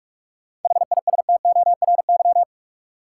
Morse code, heard as a keyed tone, about 46 wpm, 700 Hz. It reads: HISTORY
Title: Sets of 3 Words - New Word Drill - 45wpm